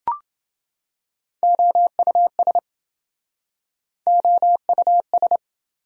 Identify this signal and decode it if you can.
Morse code, heard as a keyed tone, about 27 wpm, 700 Hz. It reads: OUS OUS